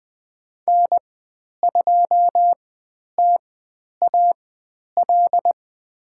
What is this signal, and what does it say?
Morse code, heard as a keyed tone, about 20 wpm, spaced as for 10 wpm, 700 Hz. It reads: N2TAL
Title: 095 Lesson 16g - Letter c - Call Signs 20wpm